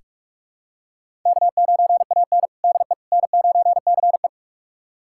Morse code, heard as a keyed tone, about 46 wpm, 700 Hz. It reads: K9AN DE N9CE
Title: USA Callsign DE Callsign - 4 Characters - 45wpm